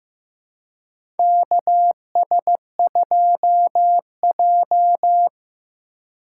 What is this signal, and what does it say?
Morse code, heard as a keyed tone, about 15 wpm, 700 Hz. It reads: KS2J
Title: USA Call Signs - 4 Characters - 2x Character Spacing - 15wpm